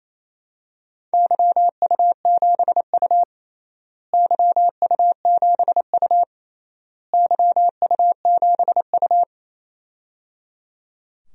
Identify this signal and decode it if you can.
Morse code, heard as a keyed tone, about 28 wpm, 700 Hz. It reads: YU7U YU7U YU7U
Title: DXCC-100 Call Signs - 4 Characters - Repeated 3x - 28wpm